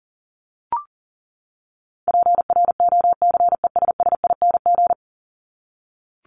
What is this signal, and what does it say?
Morse code, heard as a keyed tone, about 40 wpm, 700 Hz. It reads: PROCESSING
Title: ING Suffix - Top 100 Words with Base Words 40wpm